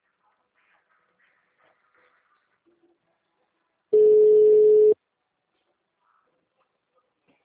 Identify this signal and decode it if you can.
Ringback tone (Europe, 425 Hz).
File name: out- -2007- - - .64